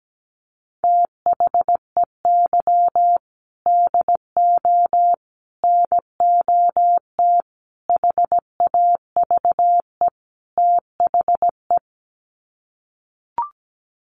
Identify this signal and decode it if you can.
Morse code, heard as a keyed tone, about 17 wpm, 700 Hz. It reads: THEY DO NOT HAVE THE
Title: Sets of 5 Words 17wpm